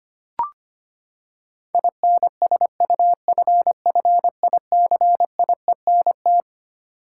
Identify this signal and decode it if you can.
Morse code, heard as a keyed tone, about 25 wpm, 700 Hz. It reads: INSUFFICIENT
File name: Twelve Letter Words - 25wpm